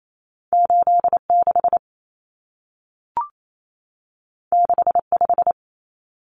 Morse code, heard as a keyed tone, about 28 wpm, 700 Hz. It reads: 86 65
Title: Numbers - 2-Digit 28wpm